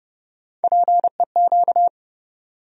Morse code, heard as a keyed tone, about 30 wpm, 700 Hz. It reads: PEQ
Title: USA Call Sign Postfixes - 3 Letters 30wpm